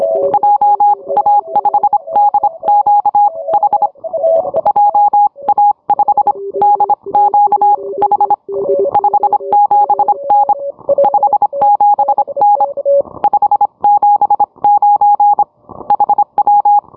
Morse code, heard as a keyed tone, about 25 wpm, spaced as for 22 wpm, 820 Hz. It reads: JA5DQH JA5DQH 57N 57N 579 HW
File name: onair